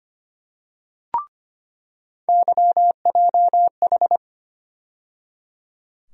Morse code, heard as a keyed tone, about 25 wpm, 700 Hz. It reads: YJH
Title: USA Call Sign Postfixes - 3 Letters 25wpm